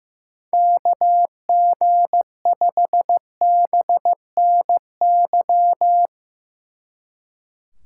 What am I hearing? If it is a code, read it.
Morse code, heard as a keyed tone, about 16 wpm, 700 Hz. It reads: KG5BNY
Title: Speed Racing - USA 2x3 Call Signs 15wpm